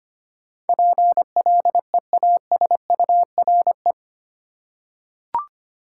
Morse code, heard as a keyed tone, about 25 wpm, 700 Hz. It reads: PLEASURE
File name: Eight Letter Words - 25wpm